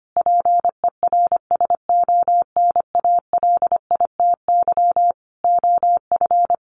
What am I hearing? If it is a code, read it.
Morse code, heard as a keyed tone, about 25 wpm, 700 Hz. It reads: PERSONALITY OF
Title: Sentences from Top 1400 Words 25wpm